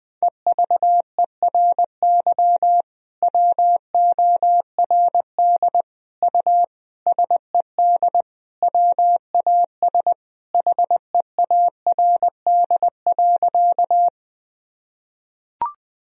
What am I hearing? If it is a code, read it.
Morse code, heard as a keyed tone, about 20 wpm, 700 Hz. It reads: EVERY WORD U SED WAS HEARD.
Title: Sentences from Top 400 Words and Abbreviations 20wpm